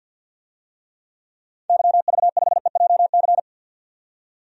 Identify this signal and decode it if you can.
Morse code, heard as a keyed tone, about 48 wpm, 700 Hz. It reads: YV5EJC
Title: DXCC-100 Call Signs - 6 Characters - 50wpm